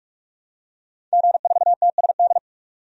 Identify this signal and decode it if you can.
Morse code, heard as a keyed tone, about 47 wpm, 700 Hz. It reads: G4TSD
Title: DXCC-100 Call Signs - 5 Characters - 45wpm